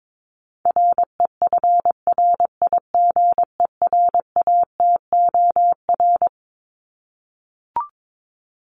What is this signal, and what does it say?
Morse code, heard as a keyed tone, about 22 wpm, 700 Hz. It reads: REFRIGERATOR
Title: Top 1501-1750 Words 22wpm